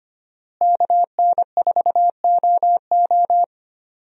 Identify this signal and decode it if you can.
Morse code, heard as a keyed tone, about 25 wpm, 700 Hz. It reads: KN4OO
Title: USA Call Signs - 5 Characters - Repeated 3x - 25wpm